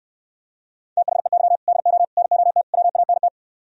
Morse code, heard as a keyed tone, about 68 wpm, 700 Hz. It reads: THEY DO NOT WANT